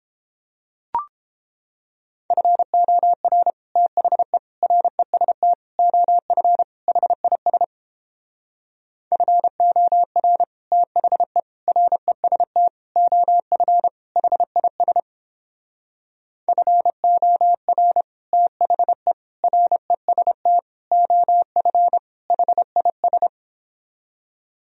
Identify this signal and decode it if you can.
Morse code, heard as a keyed tone, about 31 wpm, 700 Hz. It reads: FOR THE REST OF HIS FOR THE REST OF HIS FOR THE REST OF HIS